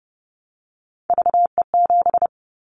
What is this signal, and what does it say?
Morse code, heard as a keyed tone, about 30 wpm, 700 Hz. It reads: VE7
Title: Callsign Prefixes for Top 100 DXCC - 30wpm